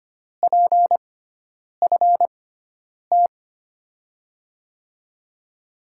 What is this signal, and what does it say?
Morse code, heard as a keyed tone, about 25 wpm, spaced as for 9 wpm, 700 Hz. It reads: PFT
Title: USA Call Signs - 5 Characters - 6x Character Spacing - 25wpm